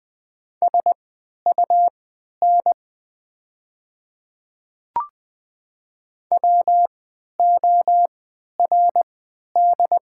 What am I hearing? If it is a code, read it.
Morse code, heard as a keyed tone, about 20 wpm, spaced as for 11 wpm, 700 Hz. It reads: SUN WORD
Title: 3x Character Spacing - Top 300 Words - 20wpm